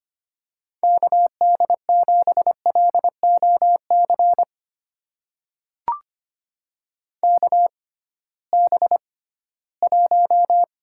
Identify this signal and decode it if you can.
Morse code, heard as a keyed tone, about 25 wpm, spaced as for 16 wpm, 700 Hz. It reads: KD7LOC KB1
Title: USA Call Signs - 6 Characters - 6x Character Spacing - 25wpm